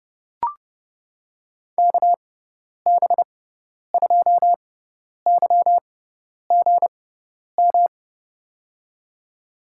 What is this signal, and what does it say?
Morse code, heard as a keyed tone, about 30 wpm, spaced as for 10 wpm, 700 Hz. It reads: KB2YGM
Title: USA Call Signs - 6 Characters - 6x Character Spacing - 30wpm